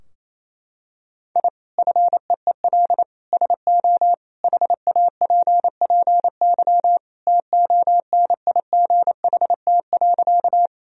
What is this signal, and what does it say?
Morse code, heard as a keyed tone, about 28 wpm, 700 Hz. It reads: I FEEL SO HAPPY TONIGHT.
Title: Sentences from Top 1000 Words - Review Part 3 - 28wpm